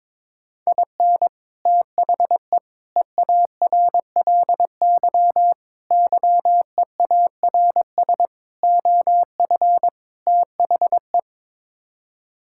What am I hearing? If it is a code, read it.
Morse code, heard as a keyed tone, about 21 wpm, 700 Hz. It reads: IN THE EARLY YEARS OF THE
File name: practice / 22wpm / Sets of 6 Words - 22wpm